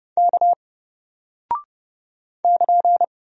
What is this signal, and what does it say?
Morse code, heard as a keyed tone, about 30 wpm, 700 Hz. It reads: K <KN>
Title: Single Character (Letter:Number:Punctuation:Prosign) Alt 30wpm